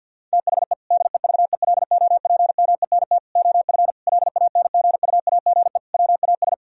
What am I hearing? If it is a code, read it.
Morse code, heard as a keyed tone, about 51 wpm, 700 Hz. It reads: THE DEVELOPMENT OF LANGUAGE WAS
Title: Sentences from Top 1000 Words - Review Part 4 - 50wpm